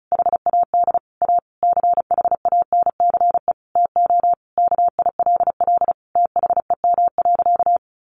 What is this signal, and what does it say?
Morse code, heard as a keyed tone, about 34 wpm, 700 Hz. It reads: HAD A CHANCE TO KILL THEM.
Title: Sentences from Top 1400 Words 35wpm